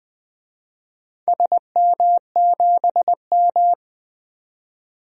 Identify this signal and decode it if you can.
Morse code, heard as a keyed tone, about 20 wpm, 700 Hz. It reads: SM7M
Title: DXCC-100 Call Signs - 4 Characters - 20wpm